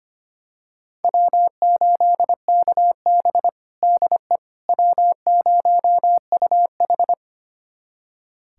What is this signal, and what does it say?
Morse code, heard as a keyed tone, about 25 wpm, 700 Hz. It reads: W8KB DE W0UH
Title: USA Callsign DE Callsign - 4 Characters - 25wpm